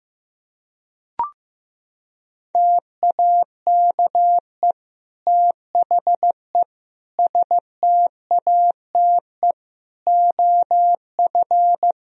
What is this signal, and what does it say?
Morse code, heard as a keyed tone, about 16 wpm, 700 Hz. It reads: TAKE THE STATE OF
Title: Sentences from Top 1000 Words - Review Part 3 - 15wpm